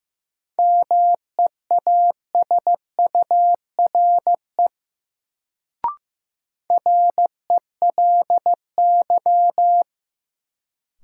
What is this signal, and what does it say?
Morse code, heard as a keyed tone, about 15 wpm, 700 Hz. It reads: MEASURE RELY
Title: Sentences from Top 1200 Words - New Word Drill Alt - 15wpm